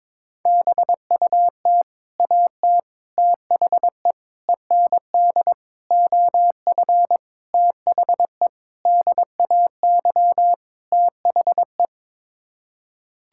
Morse code, heard as a keyed tone, about 22 wpm, 700 Hz. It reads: BUT AT THE END OF THE DAY THE
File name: practice / 22wpm / Sets of 8 Words - 22wpm